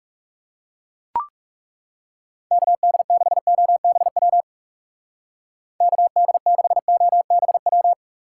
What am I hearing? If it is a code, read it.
Morse code, heard as a keyed tone, about 37 wpm, 700 Hz. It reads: KD6OBW KD6OBW
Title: Speed Racing - USA 2x3 Call Signs 30wpm